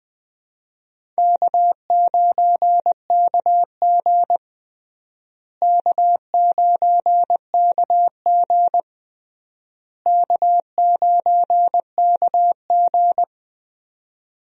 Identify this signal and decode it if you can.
Morse code, heard as a keyed tone, about 20 wpm, 700 Hz. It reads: K9KG K9KG K9KG
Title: USA Call Signs - 4 Characters - Repeated 3x - 20wpm